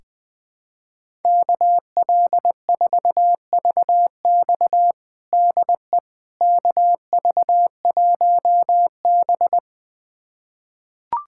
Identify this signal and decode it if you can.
Morse code, heard as a keyed tone, about 20 wpm, 700 Hz. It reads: KL4VX DE KV1B
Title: USA Callsign DE Callsign 20wpm